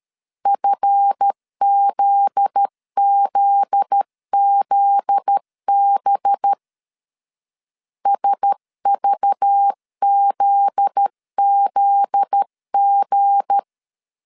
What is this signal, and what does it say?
Morse code, heard as a keyed tone, about 13 wpm, 800 Hz. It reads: FZZZB SVZZG
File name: LEZIONI / 7 GF9Z / 7 GF9Z 12wpm 2147